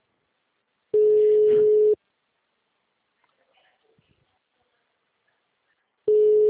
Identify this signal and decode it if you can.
Ringback tone (Europe, 425 Hz).
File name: out- -2002- - - .9716